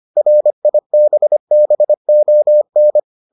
Morse code, heard as a keyed tone, about 25 wpm, 600 Hz. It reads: RIBBON